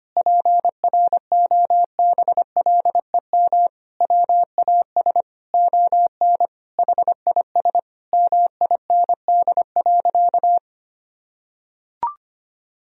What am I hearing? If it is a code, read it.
Morse code, heard as a keyed tone, about 25 wpm, 700 Hz. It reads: PROBLEM WAS ON HIS MIND.
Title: Sentences from Top 1000 Words - Review Part 2 - 25wpm